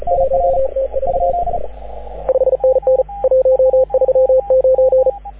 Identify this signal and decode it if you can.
Morse code, heard as a keyed tone, about 38 wpm, 530 Hz, with radio noise. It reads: 1T4 5NN 139
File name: AUDIO2019 / HG6N CW SSB MIX